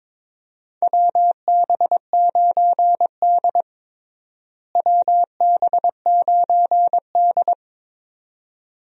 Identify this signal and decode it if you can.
Morse code, heard as a keyed tone, about 22 wpm, 700 Hz. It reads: WB9D WB9D
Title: USA Call Signs - 4 Characters - Repeated 2x - 22wpm